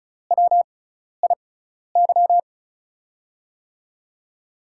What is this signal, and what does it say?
Morse code, heard as a keyed tone, about 36 wpm, spaced as for 12 wpm, 700 Hz. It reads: WIY
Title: USA Call Signs - 5 Characters - 6x Character Spacing - 35wpm